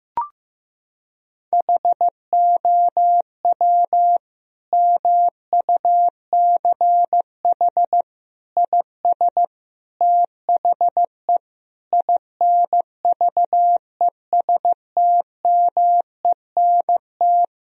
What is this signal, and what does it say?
Morse code, heard as a keyed tone, about 16 wpm, 700 Hz. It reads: HOW MUCH IS THE INVESTMENT
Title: Sentences from Top 1000 Words - Review Part 4 - 15wpm